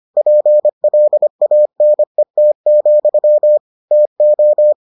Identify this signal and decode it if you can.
Morse code, heard as a keyed tone, about 25 wpm, 600 Hz. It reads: PLANET, TO